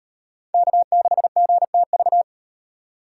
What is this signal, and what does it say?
Morse code, heard as a keyed tone, about 37 wpm, 700 Hz. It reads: K6GTV